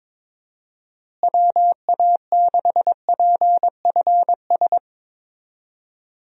Morse code, heard as a keyed tone, about 22 wpm, 700 Hz. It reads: WA6PFS
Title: USA Call Signs - 6 Characters - Repeated 2x - 22wpm